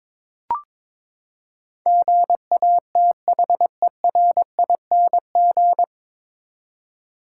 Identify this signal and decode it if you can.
Morse code, heard as a keyed tone, about 22 wpm, 700 Hz. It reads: GATHERING
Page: Morse Code Ninja practice 22wpm